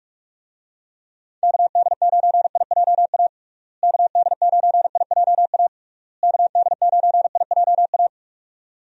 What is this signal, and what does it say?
Morse code, heard as a keyed tone, about 42 wpm, 700 Hz. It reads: KD9IJA KD9IJA KD9IJA
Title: USA Call Signs - 6 Characters - Repeated 3x - 45wpm